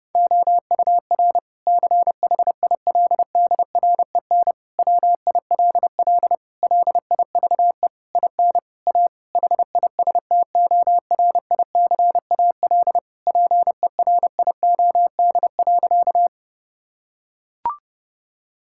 Morse code, heard as a keyed tone, about 32 wpm, 700 Hz. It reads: OUR CHILDREN WILL LIVE IN A HISTORICAL PERIOD.
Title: Sentences from Top 1000 Words - Review Part 5 - 30wpm